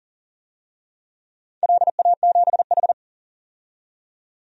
Morse code, heard as a keyed tone, about 40 wpm, 690 Hz. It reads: LA7H
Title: DXCC-100 Call Signs - 4 Characters - 40wpm